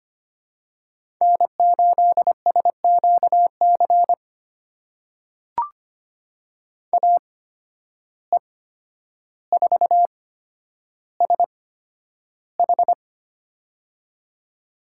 Morse code, heard as a keyed tone, about 25 wpm, spaced as for 11 wpm, 700 Hz. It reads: N8SQC AE4SH
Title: USA Call Signs - 5 Characters - 8x Character Spacing - 25wpm